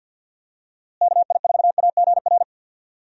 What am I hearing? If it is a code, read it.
Morse code, heard as a keyed tone, about 51 wpm, 700 Hz. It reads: KI4AGR
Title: USA Call Signs - 6 Characters - Repeated 2x - 50wpm